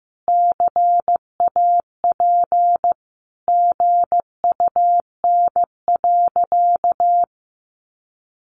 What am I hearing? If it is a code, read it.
Morse code, heard as a keyed tone, about 15 wpm, 700 Hz. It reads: CAP GUN.
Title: A Sound Of Thunder 15wpm